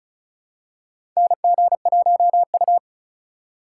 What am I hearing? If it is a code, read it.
Morse code, heard as a keyed tone, about 34 wpm, 700 Hz. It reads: NG1U
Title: USA Call Signs - 4 Characters - 4x Character Spacing - 35wpm